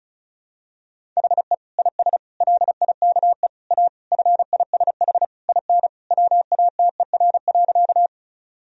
Morse code, heard as a keyed tone, about 36 wpm, 700 Hz. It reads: HE IS LIKE A FISH IN WATER.
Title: Sentences from Top 1000 Words - Review Part 1 - 35wpm